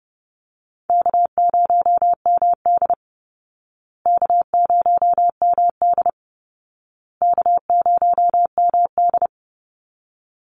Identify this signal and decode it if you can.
Morse code, heard as a keyed tone, about 28 wpm, 700 Hz. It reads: K0MD K0MD K0MD
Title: Easy Call Signs Repeated 3x - 30wpm